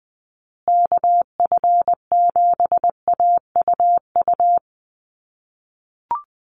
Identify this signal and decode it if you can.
Morse code, heard as a keyed tone, about 20 wpm, 700 Hz. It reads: KF7AUU
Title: USA Call Signs Repeated 3x 20wpm